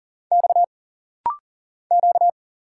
Morse code, heard as a keyed tone, about 40 wpm, 700 Hz. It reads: X Q
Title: Single Letters - Rapid-Fire 40wpm